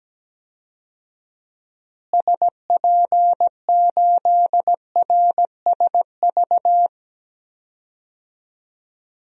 Morse code, heard as a keyed tone, about 17 wpm, 700 Hz. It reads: SP8RSV
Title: DXCC-100 Call Signs - 6 Characters - Repeated 2x - 17wpm